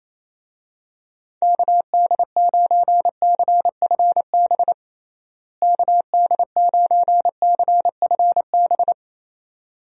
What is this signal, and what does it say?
Morse code, heard as a keyed tone, about 28 wpm, 700 Hz. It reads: KD9CFB KD9CFB
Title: USA Call Signs - 6 Characters - Repeated 2x - 28wpm